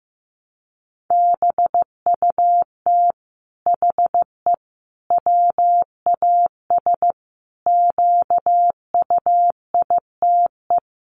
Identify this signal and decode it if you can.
Morse code, heard as a keyed tone, about 15 wpm, 700 Hz. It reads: BUT HE WAS QUITE
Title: Sentences from Top 700 Words 15wpm